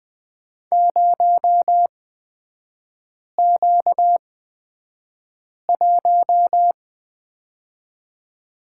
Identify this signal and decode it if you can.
Morse code, heard as a keyed tone, about 20 wpm, spaced as for 5 wpm, 690 Hz. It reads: 0Q1
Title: 236 Lesson 38e - Letter q - Three Characters 20wpm